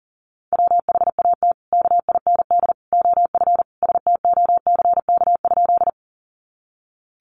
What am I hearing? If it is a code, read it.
Morse code, heard as a keyed tone, about 40 wpm, 700 Hz. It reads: WHAT KIND OF STOCK?
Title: Sentences from Top 800 Words 40wpm